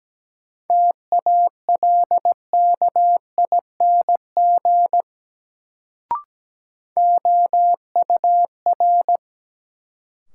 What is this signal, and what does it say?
Morse code, heard as a keyed tone, about 17 wpm, 700 Hz. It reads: TALKING OUR
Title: Sets of 2 Words - New Word Drill - 17wpm